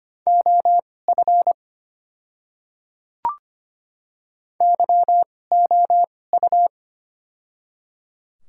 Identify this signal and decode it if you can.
Morse code, heard as a keyed tone, about 25 wpm, spaced as for 18 wpm, 690 Hz. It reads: OF YOU
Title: 2x Character Spacing - Top 100 Words - 25wpm